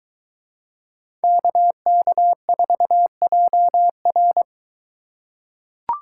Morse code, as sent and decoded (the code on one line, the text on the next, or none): -.- -.- ....- .--- .-.
KK4JR